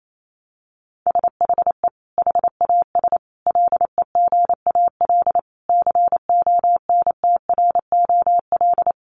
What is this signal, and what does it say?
Morse code, heard as a keyed tone, about 28 wpm, 700 Hz. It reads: SHE HAS LEGAL CONTROL
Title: Sentences from Top 1200 Words 28wpm